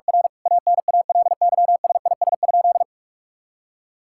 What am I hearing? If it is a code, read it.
Morse code, heard as a keyed tone, about 46 wpm, 700 Hz. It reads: R ANALYSIS?
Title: Sentences from Top 1000 Words - Review Part 5 - 45wpm